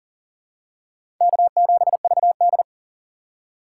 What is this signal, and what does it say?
Morse code, heard as a keyed tone, about 40 wpm, 700 Hz. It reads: K7VD
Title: USA Call Signs - 4 Characters - 6x Character Spacing - 40wpm